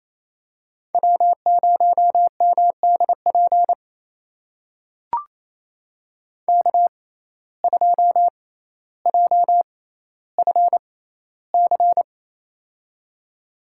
Morse code, heard as a keyed tone, about 28 wpm, spaced as for 14 wpm, 700 Hz. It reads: W0MDP K2JFC
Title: USA Call Signs - 5 Characters - 6x Character Spacing - 28wpm